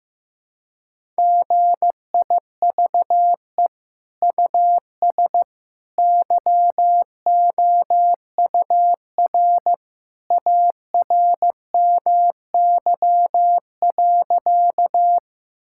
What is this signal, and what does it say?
Morse code, heard as a keyed tone, about 15 wpm, 700 Hz. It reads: GIVE US YOUR ARMY.
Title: Sentences from Top 1000 Words - Review Part 5 - 15wpm